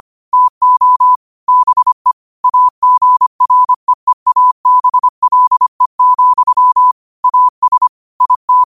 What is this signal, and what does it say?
Morse code, heard as a keyed tone, about 25 wpm, 1000 Hz. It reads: TO BE AGREEABLE, AS IT